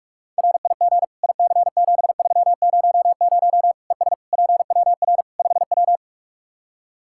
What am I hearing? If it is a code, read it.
Morse code, heard as a keyed tone, about 46 wpm, 700 Hz. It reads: RIG IC7300 ES PWR 5W
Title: Practice QSO - Exchange 2 - Info - 45wpm